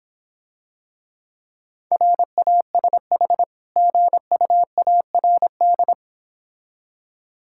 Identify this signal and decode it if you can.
Morse code, heard as a keyed tone, about 26 wpm, 690 Hz. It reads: RASH GUARD